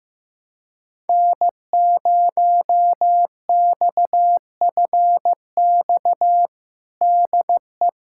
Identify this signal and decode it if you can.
Morse code, heard as a keyed tone, about 15 wpm, 700 Hz. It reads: N0XFX DE
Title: USA Callsign DE Callsign - 5 Characters - 15wpm